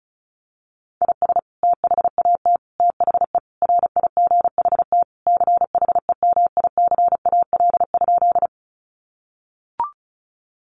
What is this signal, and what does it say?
Morse code, heard as a keyed tone, about 34 wpm, 700 Hz. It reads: IS THAT THE RIGHT CHEMICAL?
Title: Sentences from Top 1200 Words 35wpm